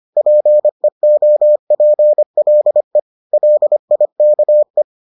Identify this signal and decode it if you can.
Morse code, heard as a keyed tone, about 25 wpm, 600 Hz. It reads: PEOPLE LIKE